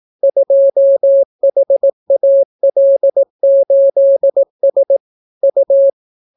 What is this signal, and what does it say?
Morse code, heard as a keyed tone, about 18 wpm, 550 Hz. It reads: 2HAL8S U